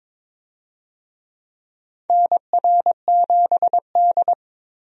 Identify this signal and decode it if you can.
Morse code, heard as a keyed tone, about 21 wpm, 700 Hz. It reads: NR7D
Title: USA Call Signs - 4 Characters - Repeated 3x - 22wpm